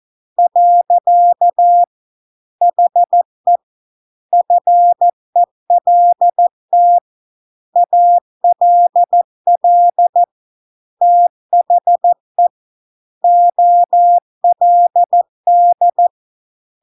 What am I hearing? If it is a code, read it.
Morse code, heard as a keyed tone, about 14 wpm, 700 Hz. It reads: . HE FELT ALL THE OLD